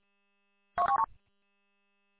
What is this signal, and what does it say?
There is no code.